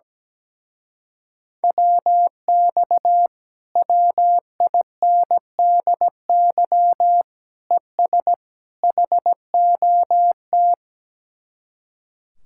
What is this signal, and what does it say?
Morse code, heard as a keyed tone, about 17 wpm, 700 Hz. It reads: WX WINDY ES HOT
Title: Speed-Racing Common QSO Phrases - 17wpm